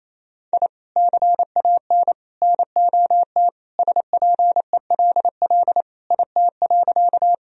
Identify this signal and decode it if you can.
Morse code, heard as a keyed tone, about 28 wpm, 700 Hz. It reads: I CAN NOT SPELL IT.
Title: Sentences from Top 1000 Words - Review Part 1 - 28wpm